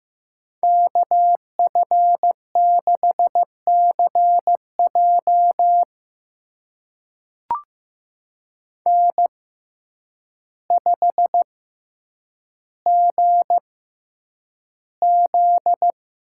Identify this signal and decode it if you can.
Morse code, heard as a keyed tone, about 15 wpm, spaced as for 8 wpm, 700 Hz. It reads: KF6CJ N5GZ
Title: USA Call Signs - 5 Characters - 6x Character Spacing - 15wpm